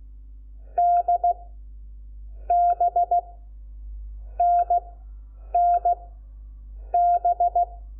Morse code, heard as a keyed tone, about 16 wpm, spaced as for 7 wpm, 700 Hz. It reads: DBNNB
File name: Lecciones / CLASE Nº5